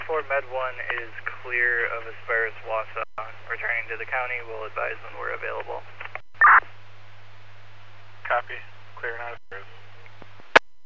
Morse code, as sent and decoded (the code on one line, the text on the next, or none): none